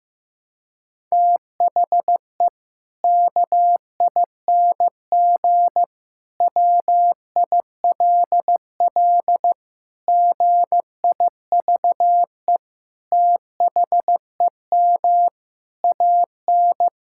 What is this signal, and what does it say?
Morse code, heard as a keyed tone, about 14 wpm, 700 Hz. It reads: THE KING WILL GIVE THEM AN
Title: Sentences from Top 1000 Words - Review Part 5 - 15wpm